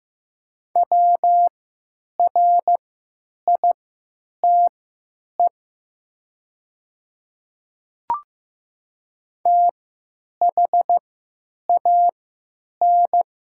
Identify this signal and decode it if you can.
Morse code, heard as a keyed tone, about 15 wpm, spaced as for 9 wpm, 700 Hz. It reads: WRITE THAN
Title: 3x Character Spacing - Top 100 Words - 15wpm